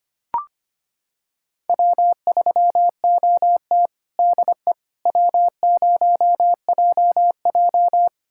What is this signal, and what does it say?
Morse code, heard as a keyed tone, about 25 wpm, 700 Hz. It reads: W3OT DE W0JJ
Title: USA Callsign DE Callsign - 4 Characters - 25wpm